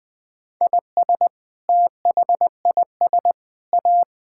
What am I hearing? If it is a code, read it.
Morse code, heard as a keyed tone, about 19 wpm, 700 Hz. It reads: IS THIS A
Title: Sentences with 7-Word Phrases - 20wpm